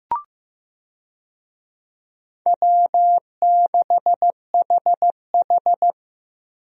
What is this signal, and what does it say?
Morse code, heard as a keyed tone, about 15 wpm, 700 Hz. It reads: W6HH
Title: USA Call Signs - 4 Characters - Repeated 2x - 15wpm